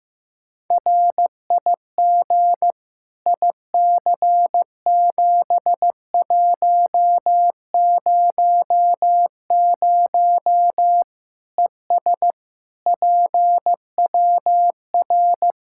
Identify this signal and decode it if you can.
Morse code, heard as a keyed tone, about 15 wpm, 700 Hz. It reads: RIG IC7100 ES PWR
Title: Practice QSO - Exchange 2 - Info - 15wpm